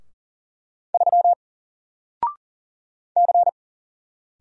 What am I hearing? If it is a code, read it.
Morse code, heard as a keyed tone, about 40 wpm, 700 Hz. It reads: 3 C
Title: Single Character (Letter:Number:Punctuation:Prosign) Alt 40wpm